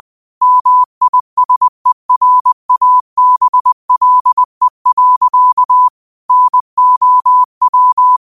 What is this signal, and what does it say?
Morse code, heard as a keyed tone, about 20 wpm, 1000 Hz. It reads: MISERABLE. NOW
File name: qotdfiles / 20wpm